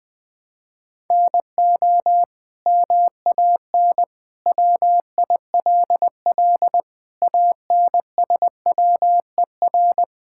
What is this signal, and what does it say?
Morse code, heard as a keyed tone, about 21 wpm, 700 Hz. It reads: NO MAN WILL ANSWER
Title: Sentences from Top 1000 Words - Review Part 6 - 20wpm